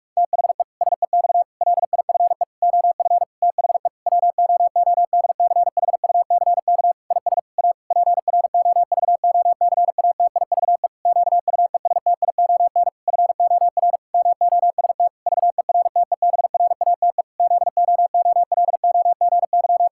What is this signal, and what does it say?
Morse code, heard as a keyed tone, about 46 wpm, 700 Hz. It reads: THE SEX LIFE OF THE WOODCHUCK IS A PROVOCATIVE QUESTION FOR MOST VERTEBRATE ZOOLOGY
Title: Pangram Challenge v2 - 45wpm